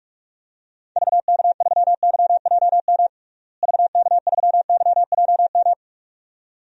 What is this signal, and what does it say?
Morse code, heard as a keyed tone, about 43 wpm, 700 Hz. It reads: VK3YJM VK3YJM
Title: DXCC-100 Call Signs - 6 Characters - Repeated 2x - 45wpm